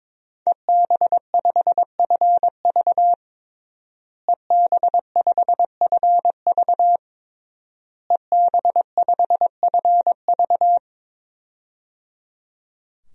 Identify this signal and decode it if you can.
Morse code, heard as a keyed tone, about 22 wpm, 700 Hz. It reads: EB5FV EB5FV EB5FV
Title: DXCC-100 Call Signs - 5 Characters - Repeated 3x - 22wpm